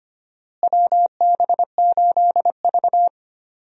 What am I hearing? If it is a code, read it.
Morse code, heard as a keyed tone, about 25 wpm, 700 Hz. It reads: WB8V
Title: USA Call Signs - 4 Characters - Repeated 3x - 25wpm